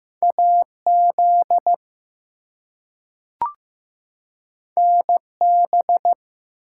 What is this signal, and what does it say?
Morse code, heard as a keyed tone, about 15 wpm, 700 Hz. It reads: AZ NB
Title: US States and Canadian Provinces 15wpm